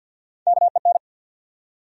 Morse code, heard as a keyed tone, about 53 wpm, 700 Hz. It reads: KEN
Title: USA Common Names - Max 3 Letters - 50wpm